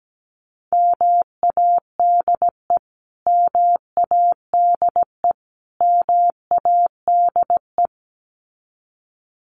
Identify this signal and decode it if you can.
Morse code, heard as a keyed tone, about 17 wpm, 700 Hz. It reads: MADE MADE MADE
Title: Top 300 Words Repeated 17wpm